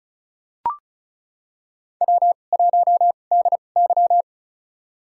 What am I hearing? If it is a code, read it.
Morse code, heard as a keyed tone, about 34 wpm, spaced as for 25 wpm, 700 Hz. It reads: W1DY
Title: USA Call Signs - 2x Character Spacing - 35wpm